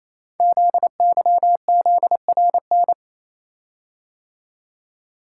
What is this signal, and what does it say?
Morse code, heard as a keyed tone, about 27 wpm, 700 Hz. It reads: ZYZRN
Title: Code Groups - 5 Letters - 28wpm